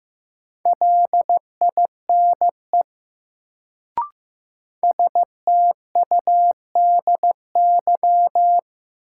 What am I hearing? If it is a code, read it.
Morse code, heard as a keyed tone, about 15 wpm, 700 Hz. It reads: LINE STUDY
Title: Sets of 5 Words - New Word Drill - 15wpm